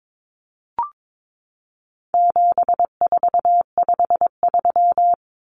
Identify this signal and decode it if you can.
Morse code, heard as a keyed tone, about 22 wpm, 700 Hz. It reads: 7453
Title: Numbers - 4-Digit 22wpm